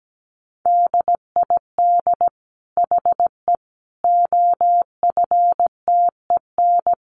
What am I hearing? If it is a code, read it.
Morse code, heard as a keyed tone, about 17 wpm, 700 Hz. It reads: DID HE OFTEN
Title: Sentences from Top 1000 Words 17wpm